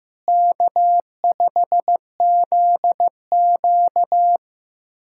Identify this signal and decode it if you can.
Morse code, heard as a keyed tone, about 16 wpm, 700 Hz. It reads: K5ZQ
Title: K1USN SST QSO Practice - 15wpm